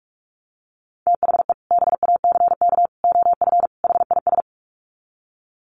Morse code, heard as a keyed tone, about 44 wpm, 700 Hz. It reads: THE BACK OF HIS
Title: Sets of 4 Words 45wpm